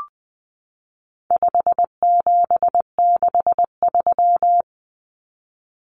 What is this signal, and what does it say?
Morse code, heard as a keyed tone, about 19 wpm, 700 Hz. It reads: 5763
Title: Numbers - 4-Digit 20wpm